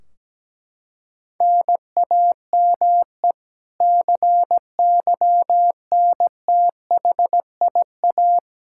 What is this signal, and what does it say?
Morse code, heard as a keyed tone, about 17 wpm, 700 Hz. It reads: NAME CYNTHIA
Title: K1USN SST QSO Practice 17wpm